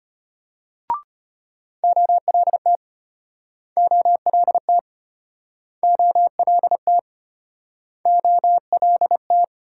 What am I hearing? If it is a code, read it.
Morse code, heard as a keyed tone, about 32 wpm, 700 Hz. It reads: OLT OLT OLT OLT